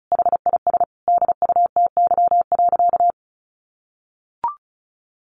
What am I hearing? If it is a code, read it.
Morse code, heard as a keyed tone, about 34 wpm, 700 Hz. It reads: HIS DUTY.
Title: Sentences from Top 1500 Words 35wpm